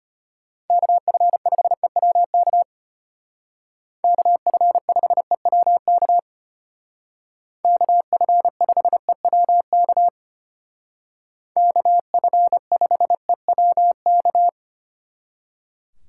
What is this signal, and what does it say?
Morse code, heard as a keyed tone, about 33 wpm, 700 Hz. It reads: KF5EWK KF5EWK KF5EWK KF5EWK